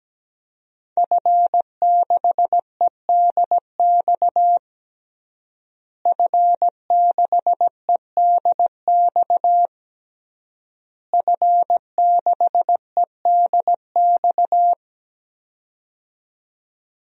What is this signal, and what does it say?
Morse code, heard as a keyed tone, about 17 wpm, 700 Hz. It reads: F6EDX F6EDX F6EDX
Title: DXCC-100 Call Signs - 5 Characters - Repeated 3x - 17wpm